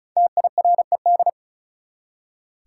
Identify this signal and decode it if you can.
Morse code, heard as a keyed tone, about 34 wpm, 700 Hz. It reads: TIRED
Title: Five Letter Words - 35wpm